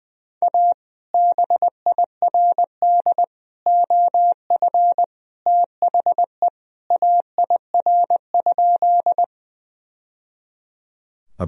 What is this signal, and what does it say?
Morse code, heard as a keyed tone, about 21 wpm, 700 Hz. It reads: A BIRD OF THE AIR?
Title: Sentences from Top 1000 Words - Review Part 2 - 20wpm